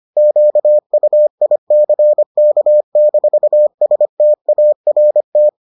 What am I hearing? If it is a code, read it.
Morse code, heard as a keyed tone, about 25 wpm, 600 Hz. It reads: QUICK-START